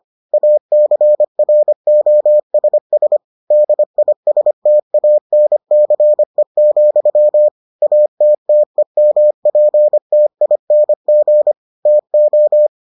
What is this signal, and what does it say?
Morse code, heard as a keyed tone, about 25 wpm, 600 Hz. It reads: ACROSS DISTANCE, ATTEMPTING TO